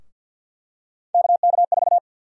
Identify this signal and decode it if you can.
Morse code, heard as a keyed tone, about 50 wpm, 700 Hz. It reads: KK4
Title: USA Call Sign Prefixes - 50wpm